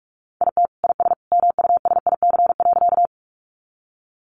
Morse code, heard as a keyed tone, about 44 wpm, 700 Hz. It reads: IT IS MUSIC.
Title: Sentences from Top 300 Words 45wpm